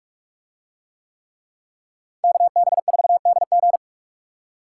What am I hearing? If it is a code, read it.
Morse code, heard as a keyed tone, about 46 wpm, 700 Hz. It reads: KB4DG